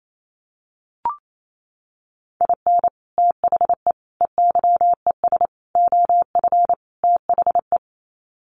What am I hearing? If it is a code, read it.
Morse code, heard as a keyed tone, about 28 wpm, 700 Hz. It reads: IN THE EYES OF THE
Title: Sets of 5 Words 28wpm